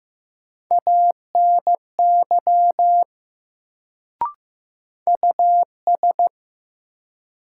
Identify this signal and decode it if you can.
Morse code, heard as a keyed tone, about 15 wpm, 700 Hz. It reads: ANY US
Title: Sentences from Top 200 Words - New Word Drill - 15wpm